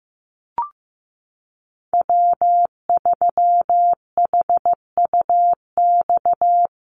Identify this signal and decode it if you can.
Morse code, heard as a keyed tone, about 15 wpm, 700 Hz. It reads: W3HUX
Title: USA Call Signs Repeated 2x 15wpm